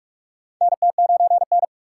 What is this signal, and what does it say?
Morse code, heard as a keyed tone, about 46 wpm, 700 Hz. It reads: NT9N
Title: USA Call Signs - 4 Characters - 4x Character Spacing - 45wpm